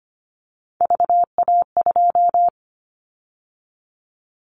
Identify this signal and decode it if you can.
Morse code, heard as a keyed tone, about 25 wpm, 700 Hz. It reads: VA2
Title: Callsign Prefixes for Top 25 DXCC - 25wpm